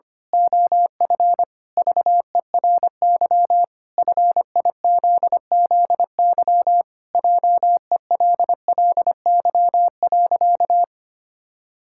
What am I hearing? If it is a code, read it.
Morse code, heard as a keyed tone, about 25 wpm, 700 Hz. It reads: OF VERY FIZZY JELLY.
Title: Pangram Challenge v2 - 25wpm